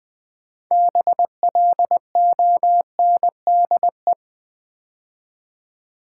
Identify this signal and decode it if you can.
Morse code, heard as a keyed tone, about 20 wpm, 700 Hz. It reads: BLONDE
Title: 6 Letter Words - First 3 Letters Given - 20wpm